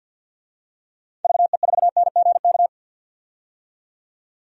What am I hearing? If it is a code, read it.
Morse code, heard as a keyed tone, about 48 wpm, 700 Hz. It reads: VE4NGK
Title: DXCC-100 Call Signs - 6 Characters - 50wpm